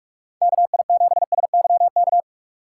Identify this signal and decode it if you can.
Morse code, heard as a keyed tone, about 56 wpm, 700 Hz. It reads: KI7SYK
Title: Speed Racing - USA 2x3 Call Signs 45wpm